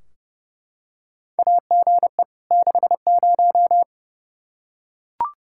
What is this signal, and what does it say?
Morse code, heard as a keyed tone, about 32 wpm, 700 Hz. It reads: AGE 60
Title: Practice QSO - Exchange 3 - More Info - 30wpm